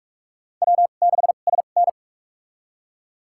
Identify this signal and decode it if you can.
Morse code, heard as a keyed tone, about 45 wpm, spaced as for 32 wpm, 690 Hz. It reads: W6SN
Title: USA Call Signs - 4 Characters - 2x Character Spacing - 45wpm